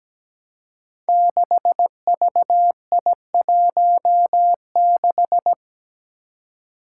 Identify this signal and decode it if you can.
Morse code, heard as a keyed tone, about 17 wpm, 700 Hz. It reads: 6VI16
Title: Five Challenging Letter-Number Combinations 17wpm